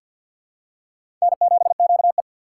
Morse code, heard as a keyed tone, about 52 wpm, 700 Hz. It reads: N7QE
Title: USA Call Signs - 3x Character Spacing - 50wpm